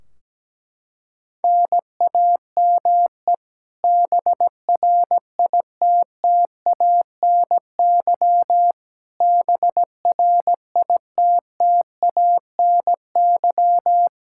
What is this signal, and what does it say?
Morse code, heard as a keyed tone, about 17 wpm, 700 Hz. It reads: NAME BRITTANY BRITTANY
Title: Practice QSO - Exchange 1 - Beginning - 17wpm